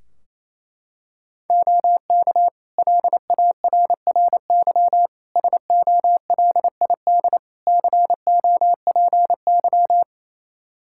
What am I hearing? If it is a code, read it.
Morse code, heard as a keyed tone, about 29 wpm, 700 Hz. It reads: OK LARRY SOLID COPY
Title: Practice QSO - Exchange 3 - More Info - 28wpm